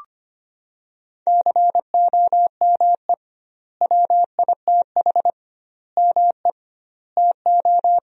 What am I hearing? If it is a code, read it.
Morse code, heard as a keyed tone, about 25 wpm, 700 Hz. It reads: COME WITH ME TO
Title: 2x Word-Spacing - Sentences from Top 200 Words - 25wpm